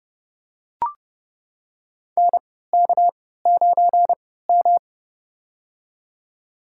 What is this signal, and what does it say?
Morse code, heard as a keyed tone, about 30 wpm, spaced as for 17 wpm, 700 Hz. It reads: NK9M
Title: USA Call Signs - 4 Characters - 3x Character Spacing - 30wpm